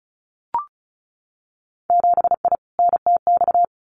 Morse code, heard as a keyed tone, about 36 wpm, 700 Hz. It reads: 7I NTX
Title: Winter Field Day Exchanges Encore 35wpm